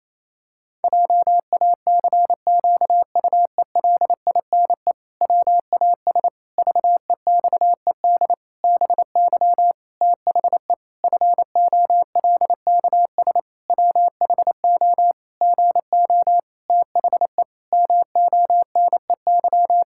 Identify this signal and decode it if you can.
Morse code, heard as a keyed tone, about 28 wpm, 700 Hz. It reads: JACQUELINE WAS VEXED BY THE FOLKS WHO GO THE MONEY
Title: Pangram Challenge v2 - 28wpm